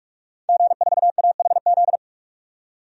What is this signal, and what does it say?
Morse code, heard as a keyed tone, about 46 wpm, 700 Hz. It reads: G4AHZ
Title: DXCC-100 Call Signs - 5 Characters - 45wpm